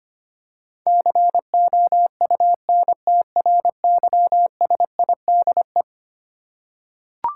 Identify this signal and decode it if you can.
Morse code, heard as a keyed tone, about 25 wpm, 700 Hz. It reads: COUNTRYSIDE
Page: morse-code-ninja-practice-25wpm